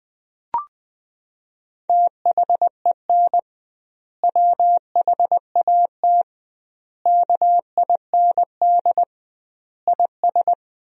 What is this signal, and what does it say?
Morse code, heard as a keyed tone, about 20 wpm, 700 Hz. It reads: THEN WHAT KIND IS
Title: 2x Word-Spacing - Sentences from Top 200 Words - 20wpm